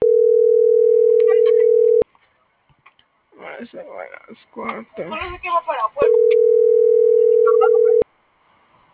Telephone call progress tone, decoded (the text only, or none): ringback tone